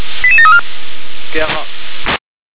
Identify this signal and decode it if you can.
Five-tone selective call (ZVEI), heard as quick runs of five tones, tones of about 70 ms, with speech. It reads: 88734